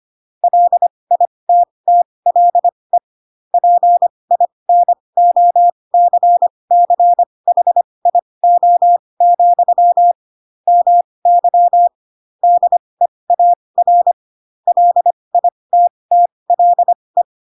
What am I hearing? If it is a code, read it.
Morse code, heard as a keyed tone, about 25 wpm, spaced as for 20 wpm, 700 Hz. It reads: LITTLE PINOCCHIO, MY DEAR LITTLE
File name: Pinocchio / Pinocchio ch7 15 25 ik1hge